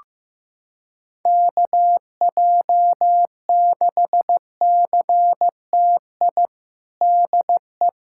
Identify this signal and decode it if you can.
Morse code, heard as a keyed tone, about 15 wpm, 700 Hz. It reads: KJ6CTI DE
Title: USA Callsign DE Callsign - 6 Characters - 15wpm